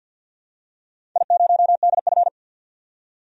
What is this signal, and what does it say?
Morse code, heard as a keyed tone, about 51 wpm, 700 Hz. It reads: I0DF
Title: DXCC-100 Call Signs - 4 Characters - 50wpm